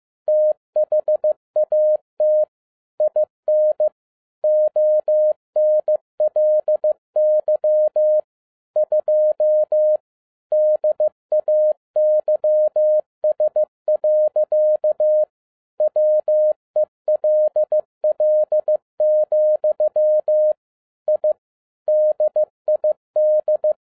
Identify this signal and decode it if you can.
Morse code, heard as a keyed tone, about 15 wpm, 610 Hz. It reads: THAT IN ONLY 2 DAYS. WELL, I DID
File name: MY CW LIFE 1 15wpm